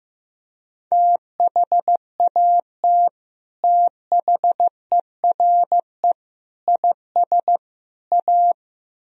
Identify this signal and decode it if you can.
Morse code, heard as a keyed tone, about 16 wpm, 700 Hz. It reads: THAT THERE IS A
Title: Speed-Racing - Sets of 4 Words 15wpm